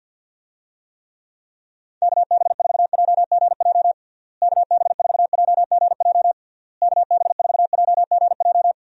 Morse code, heard as a keyed tone, about 51 wpm, 700 Hz. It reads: KB4JGJ KB4JGJ KB4JGJ
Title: USA Call Signs - 6 Characters - Repeated 3x - 50wpm